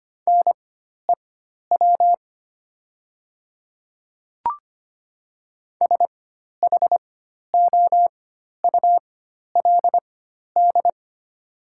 Morse code, heard as a keyed tone, about 25 wpm, spaced as for 12 wpm, 700 Hz. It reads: NEW SHOULD
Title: 4x Character Spacing - Top 300 Words - 25wpm